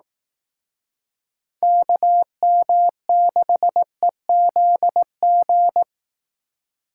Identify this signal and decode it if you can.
Morse code, heard as a keyed tone, about 18 wpm, 700 Hz. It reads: KM6EZG
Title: Speed Racing - USA 2x3 Call Signs 15wpm